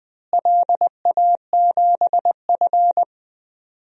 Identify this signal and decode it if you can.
Morse code, heard as a keyed tone, about 20 wpm, 700 Hz. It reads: LA7F
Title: DXCC-100 Call Signs - 4 Characters - Repeated 2x - 20wpm